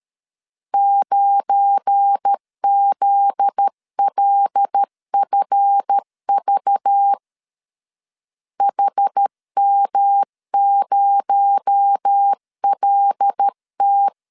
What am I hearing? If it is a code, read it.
Morse code, heard as a keyed tone, about 13 wpm, 800 Hz. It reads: 9ZLFV HM0LT
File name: LEZIONI / 7 GF9Z / 7 GF9Z 12wpm 2147